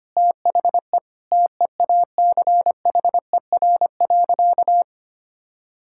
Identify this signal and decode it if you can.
Morse code, heard as a keyed tone, about 25 wpm, 700 Hz. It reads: THE TEACHER.
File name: practice / 25wpm / Sentences with 4-Word Phrases - 25wpm